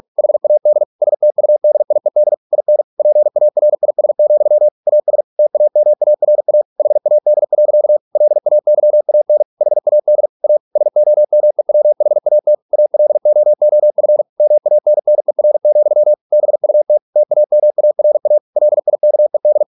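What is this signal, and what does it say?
Morse code, heard as a keyed tone, about 46 wpm, 600 Hz. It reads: HAD STUDIED IN PARIS, AS TAMARA HAD. LAYAN HAD A SOMEWHAT ALOOF MANNER, BUT TAMARA LIKED
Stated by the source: Morse practice at 45 wpm, the novel Never by Ken Follett in English, from CW med Gnister.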